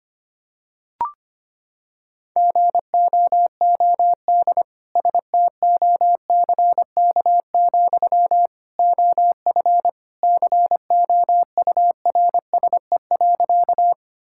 Morse code, heard as a keyed tone, about 25 wpm, 700 Hz. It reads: GOOD STOCK, OF COURSE.
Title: Sentences from Top 1000 Words - Review Part 5 - 25wpm